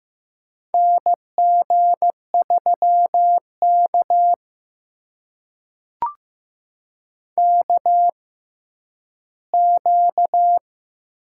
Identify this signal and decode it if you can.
Morse code, heard as a keyed tone, about 15 wpm, spaced as for 10 wpm, 700 Hz. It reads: NG3K KQ
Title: USA Call Signs - 4 Characters - 6x Character Spacing - 15wpm